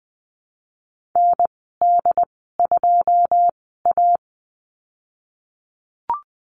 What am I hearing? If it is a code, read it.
Morse code, heard as a keyed tone, about 19 wpm, spaced as for 14 wpm, 700 Hz. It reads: ND2A
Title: 186 Lesson 30g - Letter b - Call Signs 20wpm